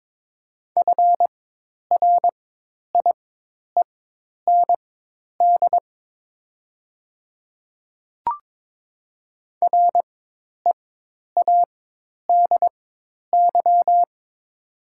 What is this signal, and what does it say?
Morse code, heard as a keyed tone, about 22 wpm, spaced as for 10 wpm, 700 Hz. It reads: FRIEND READY
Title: 4x Character Spacing - Top 500 Words - 22wpm